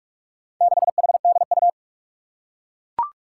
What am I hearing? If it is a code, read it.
Morse code, heard as a keyed tone, about 41 wpm, 700 Hz. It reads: 6HDU
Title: Four Challenging Letter-Number Combinations 45wpm